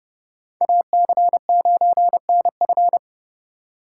Morse code, heard as a keyed tone, about 28 wpm, 700 Hz. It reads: AC9NF